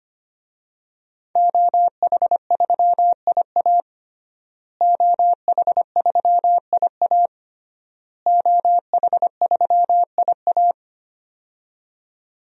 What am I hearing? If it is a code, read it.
Morse code, heard as a keyed tone, about 25 wpm, 700 Hz. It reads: OH3IA OH3IA OH3IA
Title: DXCC-100 Call Signs - 5 Characters - Repeated 3x - 25wpm